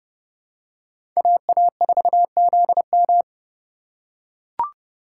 Morse code, heard as a keyed tone, about 30 wpm, 700 Hz. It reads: AA4ZM